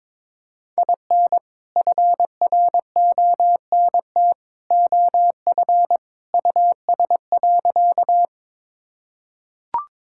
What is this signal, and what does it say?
Morse code, heard as a keyed tone, about 22 wpm, 700 Hz. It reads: IN FRONT OF US.
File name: practice / 22wpm / Sentences with 2-Word Phrases - 22wpm